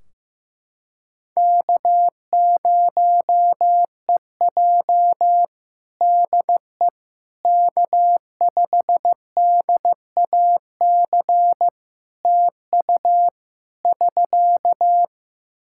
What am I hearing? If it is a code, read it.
Morse code, heard as a keyed tone, about 15 wpm, 700 Hz. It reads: K0EJ DE K5DAC TU <SK>
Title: Practice QSO - Exchange 4 - Ending - 15wpm